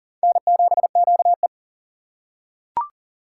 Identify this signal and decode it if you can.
Morse code, heard as a keyed tone, about 39 wpm, 700 Hz. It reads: N7QE
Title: USA Call Signs - 4 Characters - 2x Character Spacing - 40wpm